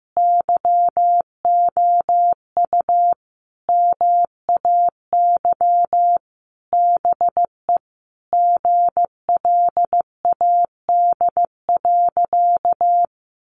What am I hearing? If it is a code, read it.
Morse code, heard as a keyed tone, about 15 wpm, 700 Hz. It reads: YOU MAY BE GLAD.
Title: Sentences from Top 800 Words 15wpm